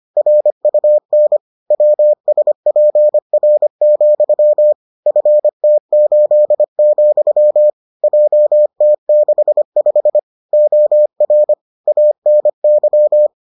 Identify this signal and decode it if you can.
Morse code, heard as a keyed tone, about 25 wpm, 600 Hz. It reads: RUN WSPR, FT8, JT65 OR ANY